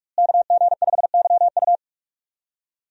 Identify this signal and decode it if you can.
Morse code, heard as a keyed tone, about 44 wpm, 700 Hz. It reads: KG5YU
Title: USA Call Signs - 5 Characters - Repeated 2x - 45wpm